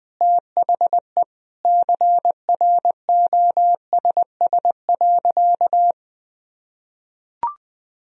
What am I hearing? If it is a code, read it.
Morse code, heard as a keyed tone, about 21 wpm, 700 Hz. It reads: THE CROSS.
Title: Sentences from Top 1000 Words - Review Part 1 - 20wpm